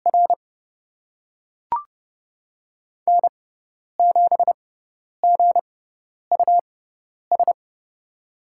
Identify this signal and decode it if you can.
Morse code, heard as a keyed tone, about 30 wpm, spaced as for 10 wpm, 700 Hz. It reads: R N7GUS
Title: USA Call Signs - 5 Characters - 6x Character Spacing - 30wpm